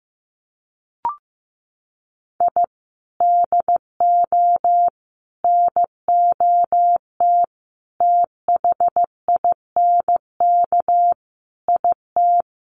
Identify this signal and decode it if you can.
Morse code, heard as a keyed tone, about 15 wpm, 700 Hz. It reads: I DO NOT THINK IT
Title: Sets of 5 Words 15wpm